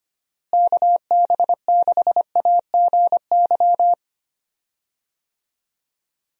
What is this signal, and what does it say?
Morse code, heard as a keyed tone, about 25 wpm, 700 Hz. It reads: KB6AGY
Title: USA Call Signs - 6 Characters - Repeated 3x - 25wpm